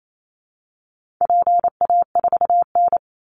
Morse code, heard as a keyed tone, about 28 wpm, 700 Hz. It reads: PA4N
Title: Call Signs Repeated 2x - 28wpm